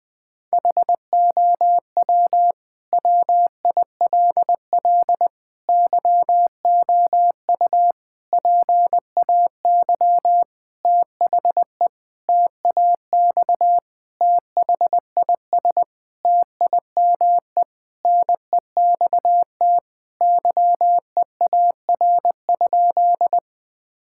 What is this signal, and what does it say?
Morse code, heard as a keyed tone, about 20 wpm, 700 Hz. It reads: HOW WILL YOU PAY THE TAX THIS TIME NEXT YEAR?
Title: Sentences from Top 1000 Words - Review Part 5 - 20wpm